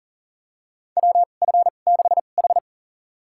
Morse code, heard as a keyed tone, about 40 wpm, spaced as for 29 wpm, 700 Hz. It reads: WF6H
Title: USA Call Signs - 4 Characters - 2x Character Spacing - 40wpm